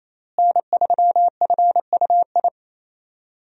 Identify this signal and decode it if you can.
Morse code, heard as a keyed tone, about 28 wpm, 700 Hz. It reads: N3FUI
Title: USA Call Signs - 5 Characters - Repeated 2x - 28wpm